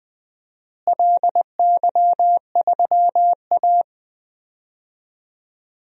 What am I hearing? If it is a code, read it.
Morse code, heard as a keyed tone, about 20 wpm, 700 Hz. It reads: LY3A
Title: DXCC-100 Call Signs - 4 Characters - Repeated 2x - 20wpm